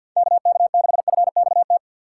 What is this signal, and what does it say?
Morse code, heard as a keyed tone, about 62 wpm, 700 Hz. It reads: KK6FXT
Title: Speed Racing - USA Call Signs 50wpm